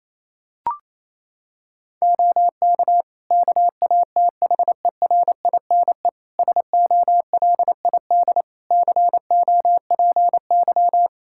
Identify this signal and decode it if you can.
Morse code, heard as a keyed tone, about 28 wpm, 700 Hz. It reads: OK KATHERINE SOLID COPY
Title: K1USN SST QSO Practice 28wpm